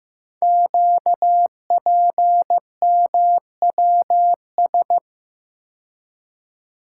Morse code, heard as a keyed tone, about 15 wpm, 700 Hz. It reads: QPMWS
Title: Code Groups - 5 Letters - 15wpm